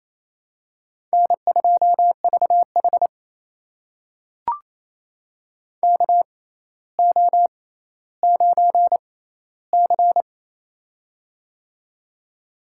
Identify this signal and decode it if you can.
Morse code, heard as a keyed tone, about 28 wpm, spaced as for 14 wpm, 700 Hz. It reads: N2VH KO9C
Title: USA Call Signs - 4 Characters - 6x Character Spacing - 28wpm